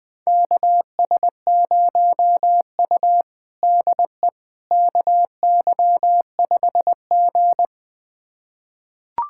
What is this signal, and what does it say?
Morse code, heard as a keyed tone, about 20 wpm, 700 Hz. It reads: KS0U DE KY5G
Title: USA Callsign DE Callsign - 4 Characters - 20wpm